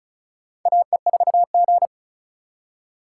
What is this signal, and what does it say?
Morse code, heard as a keyed tone, about 36 wpm, 700 Hz. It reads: AE4G
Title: USA Call Signs - 4 Characters - 4x Character Spacing - 35wpm